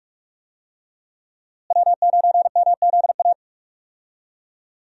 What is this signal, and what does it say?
Morse code, heard as a keyed tone, about 43 wpm, 700 Hz. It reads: W9MZA
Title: USA Call Signs - 5 Characters - Repeated 2x - 45wpm